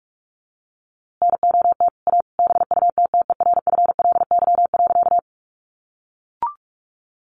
Morse code, heard as a keyed tone, about 44 wpm, 700 Hz. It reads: NOT A BUTTERFLY.
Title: A Sound Of Thunder 45wpm